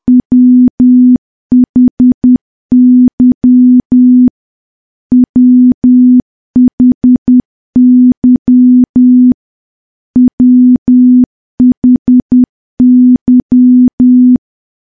Morse code, heard as a keyed tone, about 10 wpm, 250 Hz. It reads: WHY WHY WHY
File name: morse / morse-training-10WPM